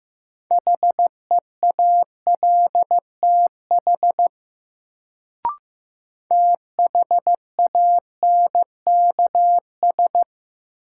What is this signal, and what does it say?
Morse code, heard as a keyed tone, about 16 wpm, 700 Hz. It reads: HEALTH THANKS
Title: Sets of 5 Words - New Word Drill - 15wpm